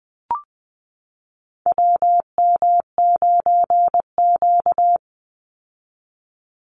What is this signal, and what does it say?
Morse code, heard as a keyed tone, about 20 wpm, 700 Hz. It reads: WM9Q
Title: Call Signs Repeated 3x - 20wpm